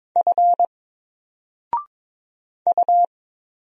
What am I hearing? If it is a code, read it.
Morse code, heard as a keyed tone, about 22 wpm, 700 Hz. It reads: F U
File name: Single-Letter-Number-22wpm